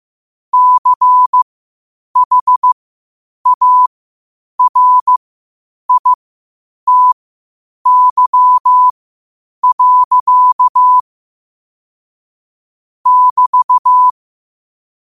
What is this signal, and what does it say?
Morse code, heard as a keyed tone, about 15 wpm, spaced as for 8 wpm, 1000 Hz. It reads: CHARITY. <BT>
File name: qotdfiles / 8wpm